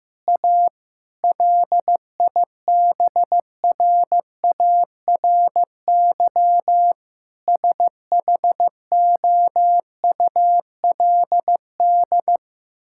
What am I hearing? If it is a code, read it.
Morse code, heard as a keyed tone, about 15 wpm, 700 Hz. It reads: A LIBRARY SHOULD
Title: Sentences from Top 1000 Words - Review Part 5 - 15wpm